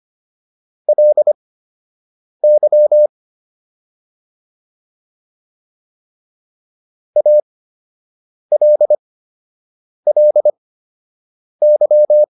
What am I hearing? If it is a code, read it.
Morse code, heard as a keyed tone, about 25 wpm, spaced as for 7 wpm, 600 Hz. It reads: LY ALLY
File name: LY4 07